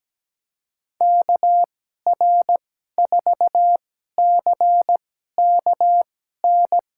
Morse code, heard as a keyed tone, about 17 wpm, spaced as for 12 wpm, 700 Hz. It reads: KR4CKN
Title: USA Call Signs - 6 Characters - 2x Character Spacing - 17wpm